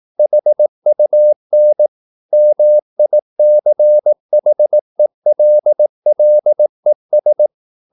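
Morse code, heard as a keyed tone, about 18 wpm, 600 Hz. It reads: HUN MICHELLES
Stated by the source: Slow (QRS) Morse practice at 18 wpm, CW med Gnister.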